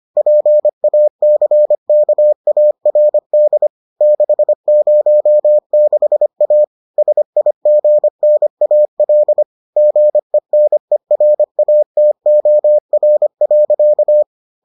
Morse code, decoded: PACKARD 606A SIGNAL GENERATOR.

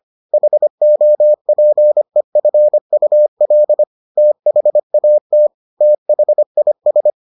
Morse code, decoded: HOPEFUL THAT THIS

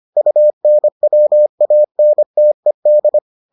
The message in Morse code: UNWANTED

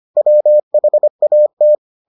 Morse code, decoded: WHAT

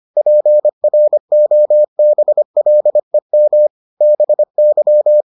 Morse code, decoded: PROBLEM BY

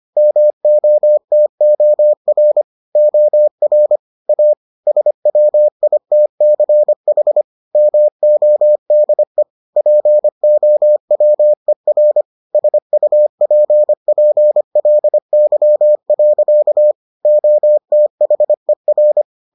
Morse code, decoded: MOTOR OR A SWITCH MODE POWER SUPPLY. OTHER

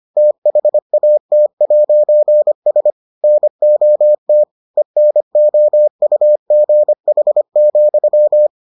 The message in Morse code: THAT'S NOT ENOUGH,

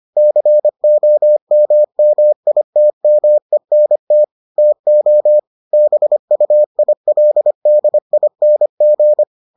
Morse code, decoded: COMMITMENT TO BUILDING

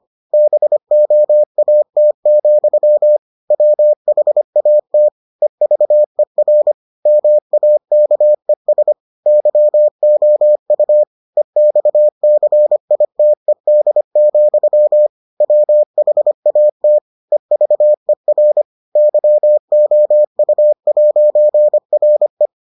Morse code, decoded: BOAT, WHAT EVER MAKES YOU EXCITED, WHAT EVER YOU'RE

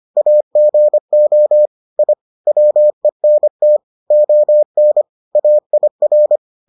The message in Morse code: AGO I WENT ON AIR

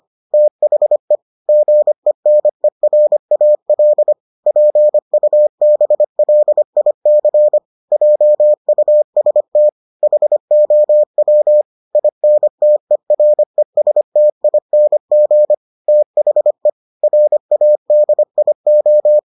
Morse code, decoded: THE GENERAL PUBLIC JUST HOW INTERESTING THE RADIO

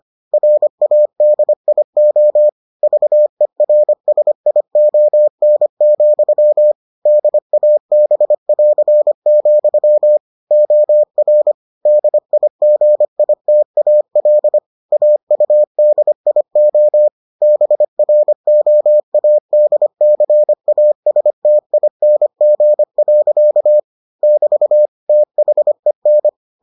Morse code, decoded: RADIO VERSION, DAB<AR>, OR DIGITAL AUDIO BROADCASTING. <BT> THEN

